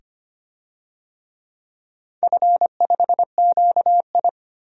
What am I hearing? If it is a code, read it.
Morse code, heard as a keyed tone, about 25 wpm, 700 Hz. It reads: F5QI
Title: DXCC-100 Call Signs - 4 Characters - 25wpm